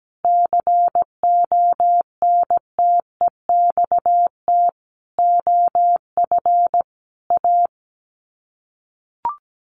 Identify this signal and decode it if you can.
Morse code, heard as a keyed tone, about 17 wpm, 700 Hz. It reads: CONTEXT OF A
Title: Sets of 5 Words Encore 17wpm